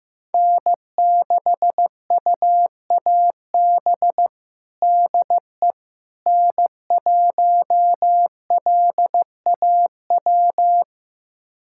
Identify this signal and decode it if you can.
Morse code, heard as a keyed tone, about 15 wpm, 700 Hz. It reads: N6UAB DE N1LAW
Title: USA Callsign DE Callsign - 5 Characters - 15wpm